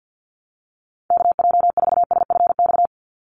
Morse code, heard as a keyed tone, about 50 wpm, 700 Hz. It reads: KJ4SFX
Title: USA Call Signs 50wpm